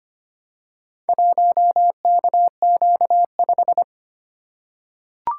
Morse code, heard as a keyed tone, about 25 wpm, 700 Hz. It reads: 1KQ5
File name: practice / 25wpm / Four Letter-Number Combinations - 25wpm